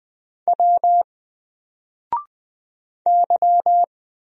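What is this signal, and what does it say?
Morse code, heard as a keyed tone, about 20 wpm, 700 Hz. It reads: W Y
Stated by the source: Single letter and number Morse drill at 20 wpm, from N4LCD.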